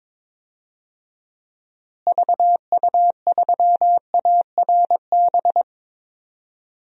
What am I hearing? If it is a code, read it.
Morse code, heard as a keyed tone, about 22 wpm, 700 Hz. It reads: VU3ARB
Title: DXCC-100 Call Signs - 6 Characters - Repeated 2x - 22wpm